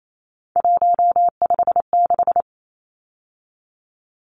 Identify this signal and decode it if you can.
Morse code, heard as a keyed tone, about 28 wpm, 700 Hz. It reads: 156
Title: Numbers - 3-Digit 28wpm